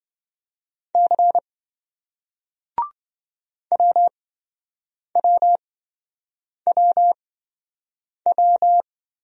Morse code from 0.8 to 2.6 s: C

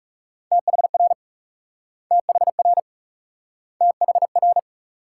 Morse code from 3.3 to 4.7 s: THR

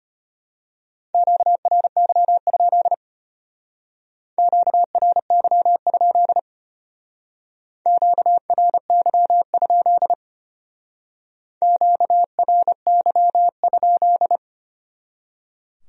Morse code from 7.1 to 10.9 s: QRY?

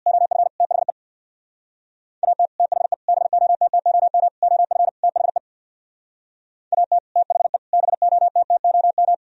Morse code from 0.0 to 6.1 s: OF THE AT THE BOTTOM OF THE

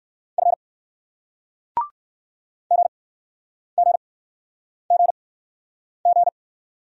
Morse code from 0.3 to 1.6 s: F

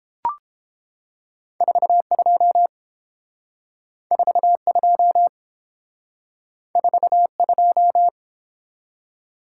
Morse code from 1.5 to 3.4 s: 42

